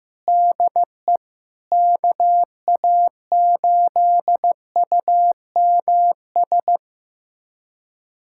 Morse code, heard as a keyed tone, about 15 wpm, 700 Hz. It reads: DE KA8UMS
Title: USA Callsign DE Callsign - 6 Characters - 15wpm